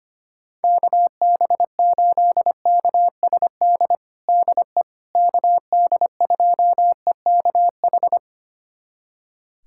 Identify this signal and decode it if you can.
Morse code, heard as a keyed tone, about 25 wpm, 700 Hz. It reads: KB8KSD DE KD2EKH
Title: USA Callsign DE Callsign 25wpm